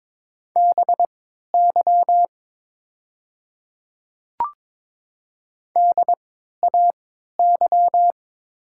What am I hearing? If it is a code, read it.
Morse code, heard as a keyed tone, about 23 wpm, spaced as for 13 wpm, 700 Hz. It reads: BY DAY
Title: 3x Character Spacing - Top 100 Words - 22wpm